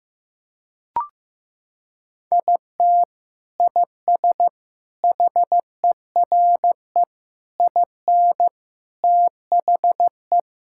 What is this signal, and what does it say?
Morse code, heard as a keyed tone, about 15 wpm, 700 Hz. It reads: IT IS HERE IN THE
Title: Sentences from Top 1600 Words 15wpm